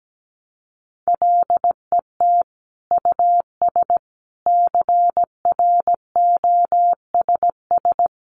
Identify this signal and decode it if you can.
Morse code, heard as a keyed tone, about 17 wpm, 700 Hz. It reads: LET US CROSS
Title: Sentences from Top 300 Words 17wpm